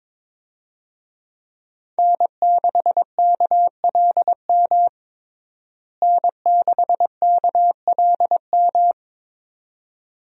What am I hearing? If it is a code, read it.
Morse code, heard as a keyed tone, about 22 wpm, 700 Hz. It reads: N6KLM N6KLM
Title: USA Call Signs - 5 Characters - Repeated 2x - 22wpm